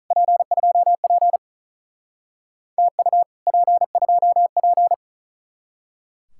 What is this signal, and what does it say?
Morse code, heard as a keyed tone, about 44 wpm, 700 Hz. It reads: P2P TU P2P